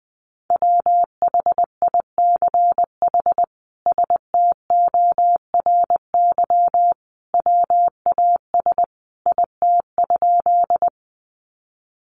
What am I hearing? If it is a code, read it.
Morse code, heard as a keyed tone, about 20 wpm, 700 Hz. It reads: WHICH STORY WAS IT?
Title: Sentences from Top 300 Words 20wpm